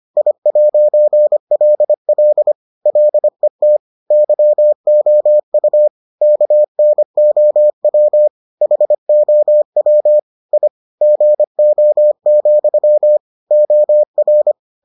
Morse code, heard as a keyed tone, about 25 wpm, 600 Hz. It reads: I'LL LET YOU KNOW HOW I GO, OR